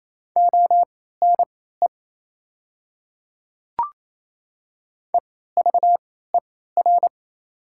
Morse code, heard as a keyed tone, about 28 wpm, spaced as for 16 wpm, 690 Hz. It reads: ONE EVER
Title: 3x Character Spacing - Top 500 Words - 28wpm